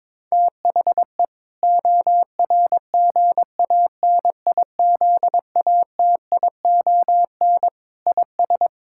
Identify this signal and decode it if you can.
Morse code, heard as a keyed tone, about 22 wpm, 700 Hz. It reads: THE ORGANIZATION IS
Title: Sentences from Top 1000 Words - Review Part 5 - 22wpm